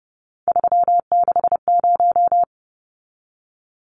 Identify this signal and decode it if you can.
Morse code, heard as a keyed tone, about 30 wpm, 700 Hz. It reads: 360
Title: Numbers - 3-Digit 30wpm